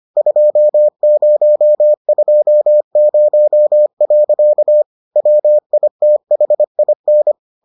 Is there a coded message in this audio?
Yes: Morse code from 0.2 to 7.3 s, heard as a keyed tone, about 25 wpm, 600 Hz. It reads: 2020. WITHIN